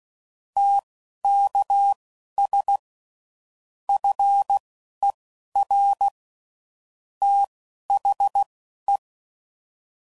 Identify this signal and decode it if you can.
Morse code, heard as a keyed tone, about 16 wpm, spaced as for 11 wpm, 800 Hz. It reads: TKS FER THE
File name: CW Code Groups @10WPM / QSOs / QSO kc7sac